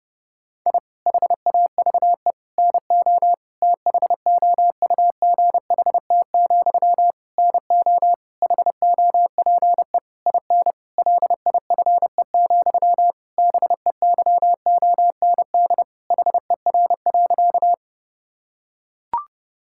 Morse code, heard as a keyed tone, about 30 wpm, 700 Hz. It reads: I HAVE NO THOUGHT, NO HOPE IN LIFE, BEYOND HER.
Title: Sentences from Top 1000 Words - Review Part 6 - 30wpm